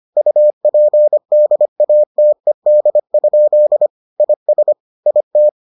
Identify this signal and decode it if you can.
Morse code, heard as a keyed tone, about 25 wpm, 600 Hz. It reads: UPDATED? IS IT